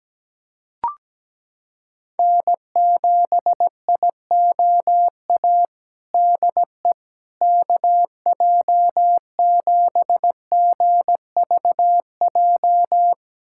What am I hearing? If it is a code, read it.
Morse code, heard as a keyed tone, about 17 wpm, 700 Hz. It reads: N7IOA DE KJ7GVJ
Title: USA Callsign DE Callsign 17wpm